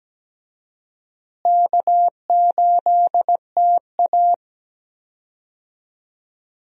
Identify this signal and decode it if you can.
Morse code, heard as a keyed tone, about 17 wpm, 700 Hz. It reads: K8TA
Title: USA Call Signs - 4 Characters - Repeated 2x - 17wpm